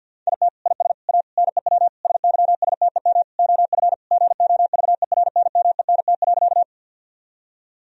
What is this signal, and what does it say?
Morse code, heard as a keyed tone, about 52 wpm, 700 Hz. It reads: IT IS A NEW SYSTEM OF GOVERNMENT.
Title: Sentences from Top 1000 Words - Review Part 3 - 50wpm